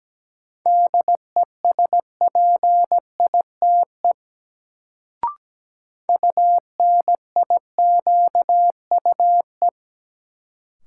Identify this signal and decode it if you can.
Morse code, heard as a keyed tone, about 17 wpm, 700 Hz. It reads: DESPITE UNIQUE
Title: Sentences from Top 1000 Words - Review Part 7 - New Word Drill - 17wpm